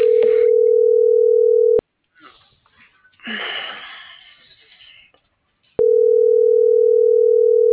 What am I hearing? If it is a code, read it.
Ringback tone (North America), 440 + 480 Hz.